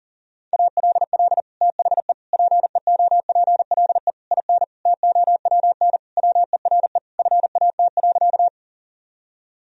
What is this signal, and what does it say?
Morse code, heard as a keyed tone, about 40 wpm, 700 Hz. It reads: ALL THE PEOPLE IN TOWN WERE FAT.
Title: Sentences from Top 1000 Words - Review Part 4 - 40wpm